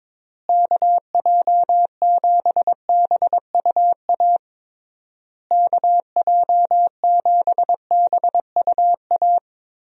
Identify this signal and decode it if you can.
Morse code, heard as a keyed tone, about 22 wpm, 700 Hz. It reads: KJ7BUA KJ7BUA
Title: USA Call Signs - 6 Characters - Repeated 3x - 22wpm